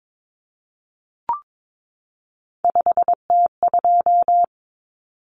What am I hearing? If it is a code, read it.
Morse code, heard as a keyed tone, about 22 wpm, 700 Hz. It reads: 5T2
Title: Common Cut Numbers - 3-Digit 22wpm